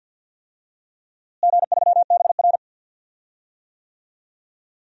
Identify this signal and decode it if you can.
Morse code, heard as a keyed tone, about 52 wpm, 700 Hz. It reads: G3BR
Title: DXCC-100 Call Signs - 4 Characters - Repeated 2x - 50wpm